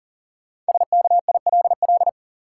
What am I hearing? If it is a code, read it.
Morse code, heard as a keyed tone, about 40 wpm, 700 Hz. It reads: SKILL
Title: Sentences from Top 1200 Words - New Word Drill - 40wpm